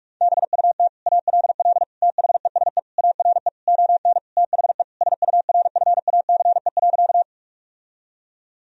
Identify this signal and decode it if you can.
Morse code, heard as a keyed tone, about 41 wpm, 700 Hz. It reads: BUT ALL THESE ARE ON THE SURFACE.
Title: Sentences from Top 1000 Words - Review Part 3 - 45wpm